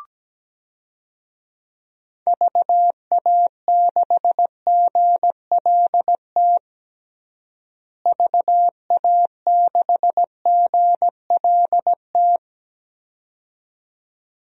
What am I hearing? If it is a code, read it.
Morse code, heard as a keyed tone, about 17 wpm, 700 Hz. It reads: VA6GLT VA6GLT
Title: DXCC-100 Call Signs - 6 Characters - Repeated 2x - 17wpm